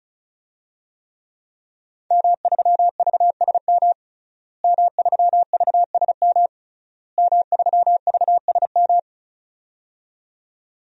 Morse code, heard as a keyed tone, about 35 wpm, 700 Hz. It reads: M3VSM M3VSM M3VSM
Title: DXCC-100 Call Signs - 5 Characters - Repeated 3x - 35wpm